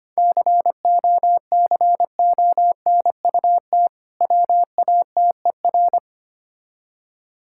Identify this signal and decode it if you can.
Morse code, heard as a keyed tone, about 25 wpm, 700 Hz. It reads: COCONUT WATER
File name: practice / 25wpm / Difficult Words with Hints - 25wpm